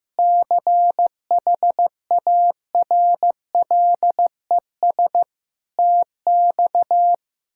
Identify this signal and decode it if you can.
Morse code, heard as a keyed tone, about 15 wpm, 700 Hz. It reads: CHARLES TX
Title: K1USN SST QSO Practice - 15wpm